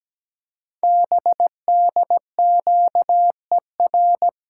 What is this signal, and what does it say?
Morse code, heard as a keyed tone, about 17 wpm, 700 Hz. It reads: BDQER
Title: Code Groups - 5 Letters - 17wpm